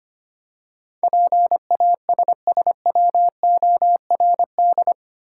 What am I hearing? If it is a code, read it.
Morse code, heard as a keyed tone, about 25 wpm, 700 Hz. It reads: PASSWORD
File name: Eight Letter Words - 25wpm